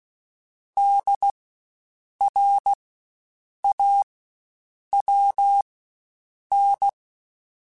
Morse code, heard as a keyed tone, about 16 wpm, spaced as for 7 wpm, 800 Hz. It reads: DRAWN
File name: Spaced Alphas / Spaced Alphas mp3s / Ds sp @15